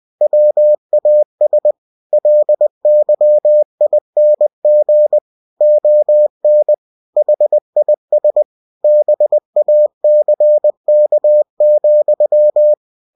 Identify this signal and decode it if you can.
Morse code, heard as a keyed tone, about 20 wpm, 600 Hz. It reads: WAS LYING ON HIS BACK,
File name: cw-books / winnie-the-pooh / winnie-20wpm-page-0085